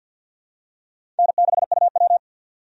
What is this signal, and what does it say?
Morse code, heard as a keyed tone, about 51 wpm, 700 Hz. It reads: N6UW
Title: USA Call Signs - 4 Characters - 50wpm